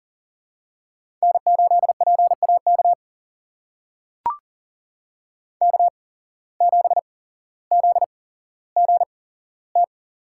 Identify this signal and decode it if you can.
Morse code, heard as a keyed tone, about 43 wpm, spaced as for 17 wpm, 700 Hz. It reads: N8PAK K7ZGT
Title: USA Call Signs - 5 Characters - 8x Character Spacing - 40wpm